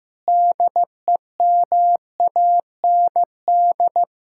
Morse code, heard as a keyed tone, about 15 wpm, 700 Hz. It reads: DEMAND
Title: Sentences from Top 1600 Words 15wpm